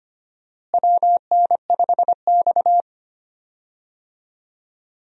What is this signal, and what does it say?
Morse code, heard as a keyed tone, about 25 wpm, 700 Hz. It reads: WN5X
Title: USA Call Signs - 4 Characters - Repeated 3x - 25wpm